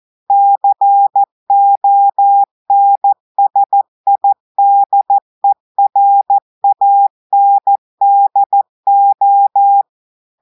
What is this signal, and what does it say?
Morse code, heard as a keyed tone, about 14 wpm, 800 Hz. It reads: CONSIDERANDO
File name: audios / portugais / portugais texte en clair 2 vitesse 14 mots minutes